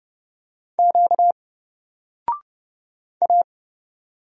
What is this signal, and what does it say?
Morse code, heard as a keyed tone, about 30 wpm, 700 Hz. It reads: Q A
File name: Single-Letter-Number-30wpm